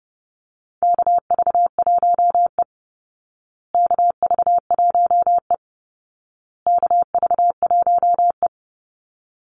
Morse code, heard as a keyed tone, about 30 wpm, 700 Hz. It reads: KV1E KV1E KV1E
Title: Call Signs Repeated 2x - 30wpm